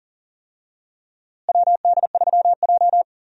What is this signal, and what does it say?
Morse code, heard as a keyed tone, about 40 wpm, 700 Hz. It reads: WD3J
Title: USA Call Signs - 4 Characters - Repeated 2x - 40wpm